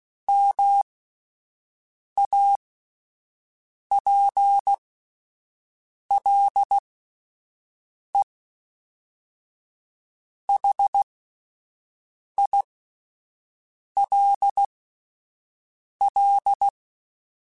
Morse code, heard as a keyed tone, about 16 wpm, spaced as for 5 wpm, 800 Hz. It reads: MAPLEHILL